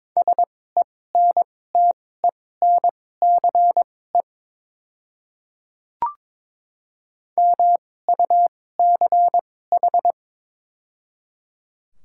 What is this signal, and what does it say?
Morse code, heard as a keyed tone, about 22 wpm, spaced as for 16 wpm, 700 Hz. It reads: SENTENCE MUCH
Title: 2x Character Spacing - Top 300 Words - 22wpm